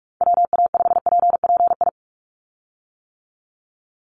Morse code, heard as a keyed tone, about 46 wpm, 700 Hz. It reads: WA5PPI
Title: USA Call Signs 45wpm